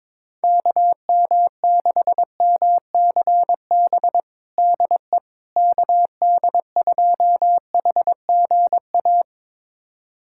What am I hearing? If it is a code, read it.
Morse code, heard as a keyed tone, about 22 wpm, 700 Hz. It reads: KM6MCB DE KD2HGA
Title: USA Callsign DE Callsign 22wpm